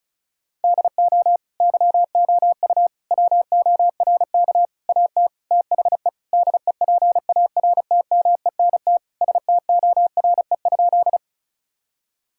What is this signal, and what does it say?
Morse code, heard as a keyed tone, about 36 wpm, 700 Hz. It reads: DO YOU WORK AT THE DEPARTMENT STORE?
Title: Sentences from Top 1000 Words - Review Part 5 - 35wpm